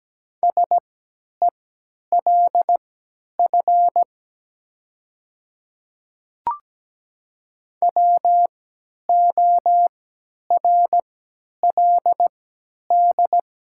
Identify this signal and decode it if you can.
Morse code, heard as a keyed tone, about 17 wpm, spaced as for 10 wpm, 700 Hz. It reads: SELF WORLD
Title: 3x Character Spacing - Top 200 Words - 17wpm